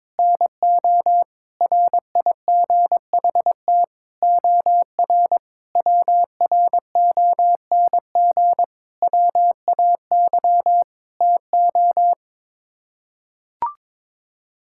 Morse code, heard as a keyed tone, about 22 wpm, 700 Hz. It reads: NO RIGHT OR WRONG WAY TO
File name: practice / 22wpm / Sets of 8 Words - 22wpm